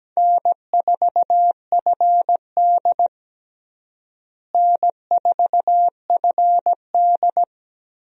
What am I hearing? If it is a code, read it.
Morse code, heard as a keyed tone, about 17 wpm, 700 Hz. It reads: N4FD N4FD
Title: USA Call Signs - 4 Characters - Repeated 2x - 17wpm